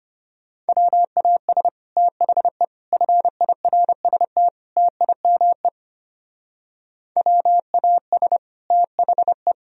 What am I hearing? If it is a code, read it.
Morse code, heard as a keyed tone, about 31 wpm, 700 Hz. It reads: WAS THE FIRST TIME WAS THE
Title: Speed-Racing - Sets of 4 Words 25wpm